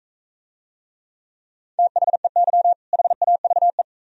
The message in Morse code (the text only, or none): THEY HAVE